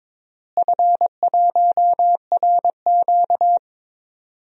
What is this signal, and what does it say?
Morse code, heard as a keyed tone, about 21 wpm, 700 Hz. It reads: F1RQ
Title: DXCC-100 Call Signs - 4 Characters - Repeated 2x - 22wpm